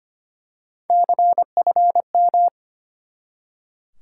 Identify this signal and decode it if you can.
Morse code, heard as a keyed tone, about 25 wpm, 700 Hz. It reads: CFM
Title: Top 90 QSO Elements 25wpm